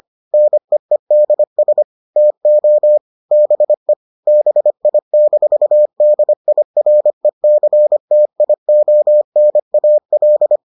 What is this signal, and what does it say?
Morse code, heard as a keyed tone, about 25 wpm, 600 Hz. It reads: NEEDS TO BE BI-DIRECTIONAL